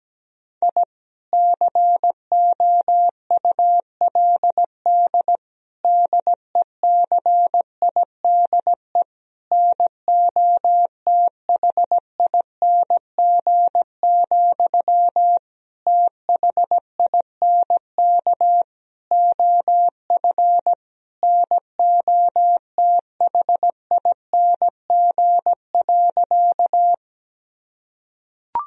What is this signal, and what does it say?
Morse code, heard as a keyed tone, about 17 wpm, 700 Hz. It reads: I COULD DECIDE NOTHING, THINK OF NOTHING.
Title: Sentences from Top 1000 Words - Review Part 3 - 17wpm